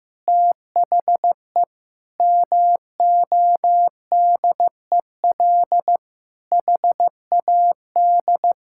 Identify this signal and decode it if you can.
Morse code, heard as a keyed tone, about 15 wpm, 700 Hz. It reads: THE MODEL HAD
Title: Sentences from Top 1000 Words - Review Part 5 - 15wpm